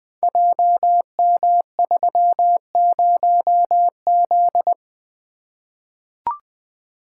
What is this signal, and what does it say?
Morse code, heard as a keyed tone, about 21 wpm, 700 Hz. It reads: JM30Z
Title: Code Groups - 5 Letter:Numbers - 20wpm